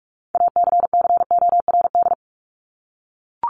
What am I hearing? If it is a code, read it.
Morse code, heard as a keyed tone, about 44 wpm, 700 Hz. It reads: ACCORD
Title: ING Suffix - Top 100 Words with Base Words 45wpm